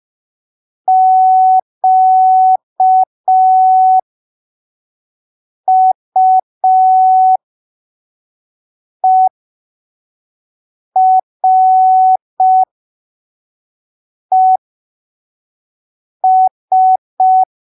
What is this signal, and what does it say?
Morse code, heard as a keyed tone, about 5 wpm, spaced as for 3 wpm, 750 Hz. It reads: QUERES
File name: arquivos / texto / texto 26 05ppm